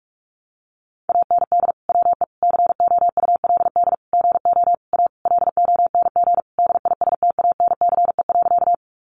Morse code, heard as a keyed tone, about 44 wpm, 700 Hz. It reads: AND WE COULD GO A LONG DISTANCE.
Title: Sentences from Top 1200 Words 45wpm